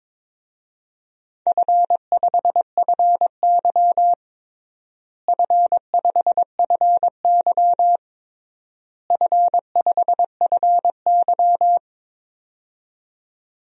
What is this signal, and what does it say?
Morse code, heard as a keyed tone, about 22 wpm, 700 Hz. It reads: F5FY F5FY F5FY
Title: DXCC-100 Call Signs - 4 Characters - Repeated 3x - 22wpm